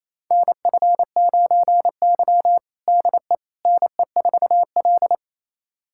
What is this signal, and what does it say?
Morse code, heard as a keyed tone, about 27 wpm, 700 Hz. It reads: NF9Y DE NE4L
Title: USA Callsign DE Callsign - 4 Characters - 28wpm